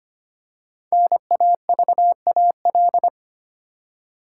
Morse code, heard as a keyed tone, about 25 wpm, 700 Hz. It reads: NAVAL
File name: 5 Letter Words - First 3 Letters Given - 25wpm